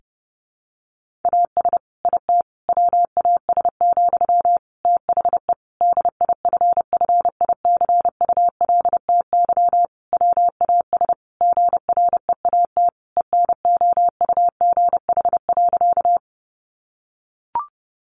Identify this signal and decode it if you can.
Morse code, heard as a keyed tone, about 30 wpm, 700 Hz. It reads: AS IT WAS, THE DIFFICULTY WAS GREAT ENOUGH.
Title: Sentences from Top 1500 Words 30wpm